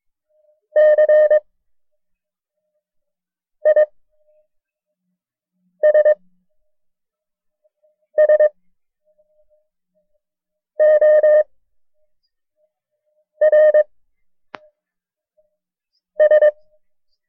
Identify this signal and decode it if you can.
Morse code, heard as a keyed tone, about 22 wpm, 610 Hz. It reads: C I S S O R S
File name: FreeFiles / CW-practice / KB6MT-tape1-side2